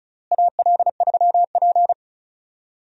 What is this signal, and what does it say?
Morse code, heard as a keyed tone, about 36 wpm, 700 Hz. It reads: AL3P
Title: USA Call Signs - 4 Characters - Repeated 3x - 35wpm